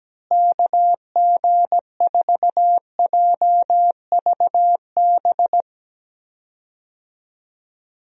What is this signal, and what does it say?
Morse code, heard as a keyed tone, about 17 wpm, 700 Hz. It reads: KG4JVB
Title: USA Call Signs - 6 Characters - Repeated 2x - 17wpm